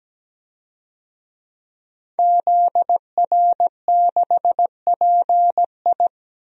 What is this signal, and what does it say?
Morse code, heard as a keyed tone, about 17 wpm, 700 Hz. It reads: ZR6PI
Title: DXCC-100 Call Signs - 5 Characters - Repeated 2x - 17wpm